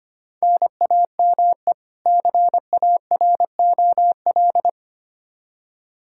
Morse code, heard as a keyed tone, about 25 wpm, 700 Hz. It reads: NAME CAROL
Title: Practice QSO - Exchange 1 - Beginning - 25wpm